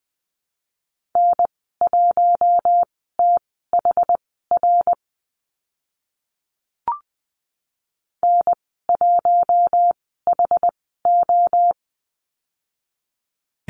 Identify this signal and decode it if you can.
Morse code, heard as a keyed tone, about 20 wpm, spaced as for 14 wpm, 700 Hz. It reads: N1THR N1HO
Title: 069 Lesson 12g - Letter d - Call Signs 20wpm